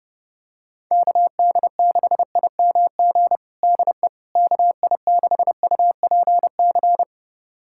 Morse code, heard as a keyed tone, about 30 wpm, 700 Hz. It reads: KD6IMG DE KI6UPC
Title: USA Callsign DE Callsign 30wpm